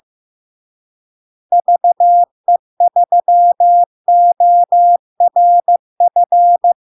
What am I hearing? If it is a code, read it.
Morse code, heard as a keyed tone, about 15 wpm, 700 Hz. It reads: VE3ORF
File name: LARK-FD-2022-Log-in-CW-15wpm-700-Hz